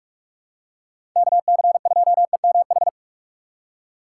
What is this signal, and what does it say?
Morse code, heard as a keyed tone, about 46 wpm, 700 Hz. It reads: KC2EMH
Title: USA Call Signs - 6 Characters - 2x Character Spacing - 45wpm